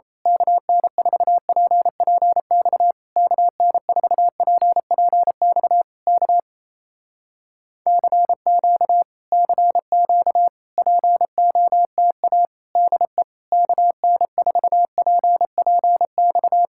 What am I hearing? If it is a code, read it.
Morse code, heard as a keyed tone, about 26 wpm, 700 Hz. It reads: KN4PPX KN4PPX K CQ CQ POTA DE KN4PPX
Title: Speed-Racing Common QSO Phrases - 28wpm